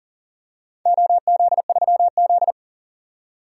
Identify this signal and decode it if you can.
Morse code, heard as a keyed tone, about 40 wpm, 700 Hz. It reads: OZ3Z
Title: DXCC-100 Call Signs - 4 Characters - 40wpm